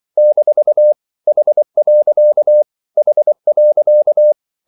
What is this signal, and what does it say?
Morse code, heard as a keyed tone, about 24 wpm, 600 Hz. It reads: - H. H.
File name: CW / Practice / EnglishQuotations0224